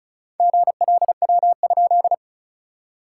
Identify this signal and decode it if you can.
Morse code, heard as a keyed tone, about 34 wpm, 690 Hz. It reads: GLW?
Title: SOTA QSO Practice 35wpm